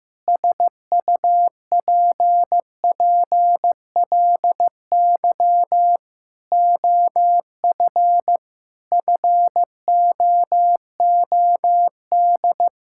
Morse code, heard as a keyed tone, about 15 wpm, 700 Hz. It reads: SUPPLY OF FOOD
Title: Sentences from Top 1000 Words - Review Part 5 - 15wpm